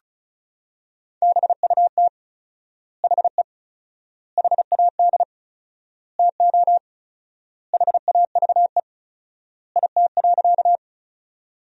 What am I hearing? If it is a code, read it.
Morse code, heard as a keyed tone, about 36 wpm, 700 Hz. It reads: BUT HE HAD TO HAVE IT.
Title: 4x Word-Spacing - Sentences from Top 100 Words - 35wpm